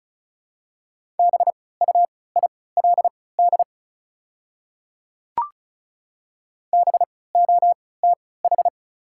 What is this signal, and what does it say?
Morse code, heard as a keyed tone, about 34 wpm, spaced as for 20 wpm, 700 Hz. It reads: BUILD BOTH
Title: 3x Character Spacing - Top 300 Words - 35wpm